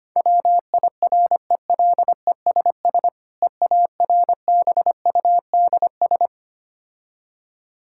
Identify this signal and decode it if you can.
Morse code, heard as a keyed tone, about 25 wpm, 700 Hz. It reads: WIRELESS EARBUDS
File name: practice / 25wpm / Difficult Words with Hints - 25wpm